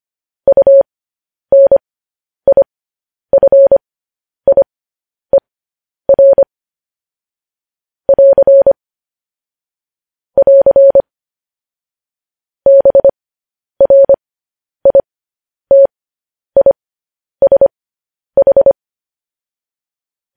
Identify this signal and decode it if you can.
Morse code, heard as a keyed tone, about 25 wpm, spaced as for 10 wpm, 550 Hz. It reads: UNIFIER <AR> <AR> BRITISH
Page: news-headlines-in-morse-code-at-10-wpm